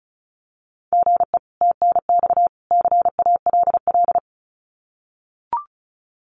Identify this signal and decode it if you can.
Morse code, heard as a keyed tone, about 34 wpm, 700 Hz. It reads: GE TNX CALL
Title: Common QSO Phrases 35wpm